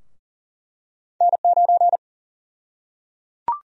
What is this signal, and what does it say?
Morse code, heard as a keyed tone, about 38 wpm, 700 Hz. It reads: N9
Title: USA Call Sign Prefixes - Repeated 2x - 40wpm